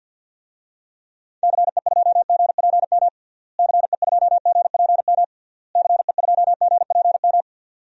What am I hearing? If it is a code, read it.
Morse code, heard as a keyed tone, about 52 wpm, 700 Hz. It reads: CE2GPM CE2GPM CE2GPM
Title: DXCC-100 Call Signs - 6 Characters - Repeated 3x - 50wpm